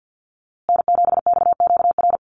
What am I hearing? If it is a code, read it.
Morse code, heard as a keyed tone, about 50 wpm, 700 Hz. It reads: N7XQR
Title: USA Call Signs Repeated 2x 50wpm